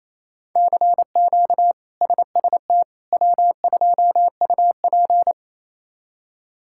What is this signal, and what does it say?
Morse code, heard as a keyed tone, about 27 wpm, 700 Hz. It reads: CQ SST W2UP
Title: K1USN SST QSO Practice - 28wpm